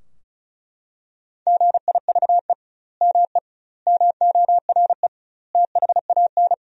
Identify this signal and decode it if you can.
Morse code, heard as a keyed tone, about 34 wpm, 700 Hz. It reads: GIVE ME MORE THAN
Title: 2x Word-Spacing - Sentences from Top 200 Words - 35wpm